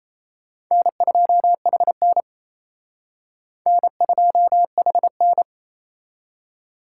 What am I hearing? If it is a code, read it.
Morse code, heard as a keyed tone, about 36 wpm, 700 Hz. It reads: N2HN N2HN